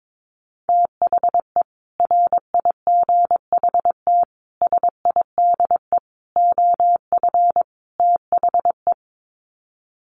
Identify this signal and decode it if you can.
Morse code, heard as a keyed tone, about 22 wpm, 700 Hz. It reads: THE RIGHT SIDE OF THE
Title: Sets of 5 Words Encore 22wpm